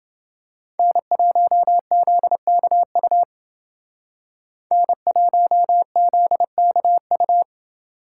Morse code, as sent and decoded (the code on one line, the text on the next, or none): -. .---- --.. -.- ..- / -. .---- --.. -.- ..-
N1ZKU N1ZKU